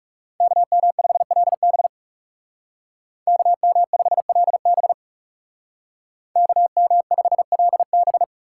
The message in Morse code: KM5LB KM5LB KM5LB

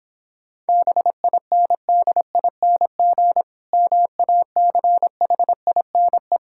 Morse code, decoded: BINDING MACHINE

